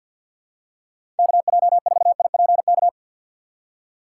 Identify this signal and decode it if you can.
Morse code, heard as a keyed tone, about 53 wpm, 700 Hz. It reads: KJ4IPK
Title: USA Call Signs - 6 Characters - 4x Character Spacing - 50wpm